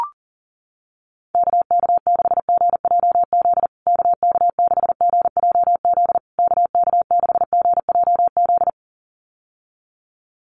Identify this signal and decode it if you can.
Morse code, heard as a keyed tone, about 37 wpm, 700 Hz. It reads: KK6GJZ KK6GJZ KK6GJZ
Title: USA Call Signs Repeated 3x 40wpm